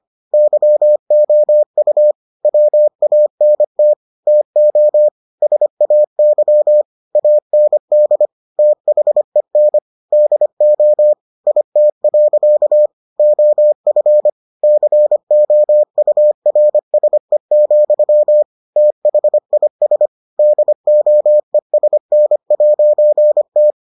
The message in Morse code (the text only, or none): YOU WANT TO SAY AND THEN DO IT. OF COURSE, THIS DOESN'T